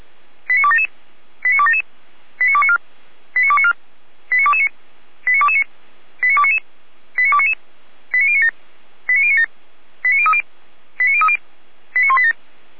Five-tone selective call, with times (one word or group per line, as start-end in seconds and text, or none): 0.5-0.9 89277
1.4-1.8 89277
2.4-2.8 89294
3.4-3.7 89294
4.3-4.7 89229
5.3-5.6 89229
6.2-6.6 89299
7.2-7.5 89299
8.1-8.5 89097
9.1-9.5 89097
10.0-10.4 89030
11.0-11.4 89030
12.0-12.3 89187